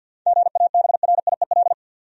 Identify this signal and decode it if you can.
Morse code, heard as a keyed tone, about 53 wpm, 700 Hz. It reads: GABRIEL
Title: USA Common Names - 50wpm